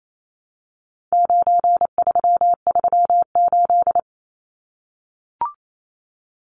Morse code, heard as a keyed tone, about 28 wpm, 700 Hz. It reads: 9338
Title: Numbers - 4-Digit 28wpm